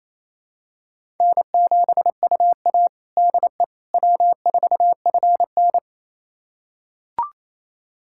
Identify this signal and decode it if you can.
Morse code, heard as a keyed tone, about 28 wpm, 700 Hz. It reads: N7UA DE W4FN
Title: USA Callsign DE Callsign - 4 Characters - 28wpm